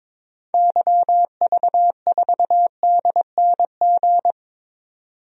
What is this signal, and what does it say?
Morse code, heard as a keyed tone, about 22 wpm, 700 Hz. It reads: YV4DNG
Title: DXCC-100 Call Signs - 6 Characters - 22wpm